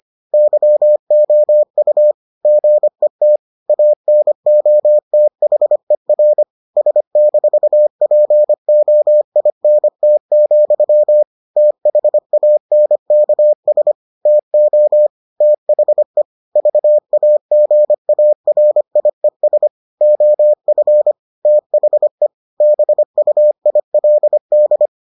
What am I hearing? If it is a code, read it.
Morse code, heard as a keyed tone, about 25 wpm, 600 Hz. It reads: YOU GET ANOTHER S-POINT, THANKS TO THE VAGARIES OF THE BUILD